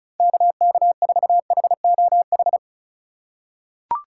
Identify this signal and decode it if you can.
Morse code, heard as a keyed tone, about 36 wpm, 700 Hz. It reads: KK4HOH
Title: USA Call Signs - 6 Characters - 2x Character Spacing - 35wpm